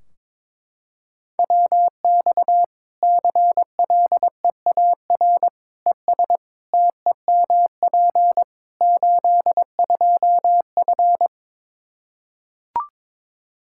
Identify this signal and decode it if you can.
Morse code, heard as a keyed tone, about 22 wpm, 700 Hz. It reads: WX CLEAR ES TEMP 82F
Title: Practice QSO - Exchange 2 - Info - 22wpm